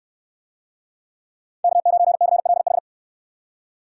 Morse code, heard as a keyed tone, about 62 wpm, 700 Hz. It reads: K9CLH